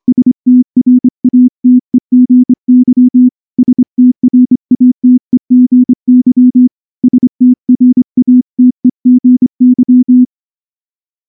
Morse code, heard as a keyed tone, about 25 wpm, 250 Hz. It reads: STRATEGY STRATEGY STRATEGY
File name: morse / morse-training-25WPM